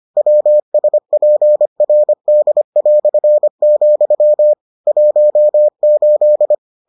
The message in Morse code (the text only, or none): WSPRD", 18